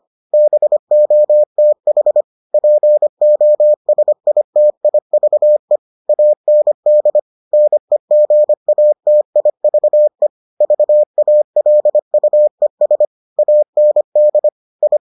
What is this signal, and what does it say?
Morse code, heard as a keyed tone, about 25 wpm, 600 Hz. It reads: BOTH POSITIVE AND NEGATIVE VALUES AND I